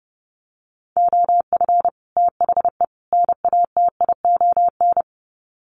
Morse code, heard as a keyed tone, about 30 wpm, 700 Hz. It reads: OF THE NATION
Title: Sets of 3 Words 30wpm